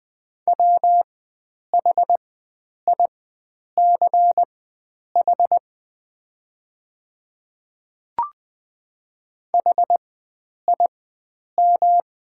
Morse code, heard as a keyed tone, about 20 wpm, spaced as for 9 wpm, 690 Hz. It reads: WHICH HIM
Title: 4x Character Spacing - Top 100 Words - 20wpm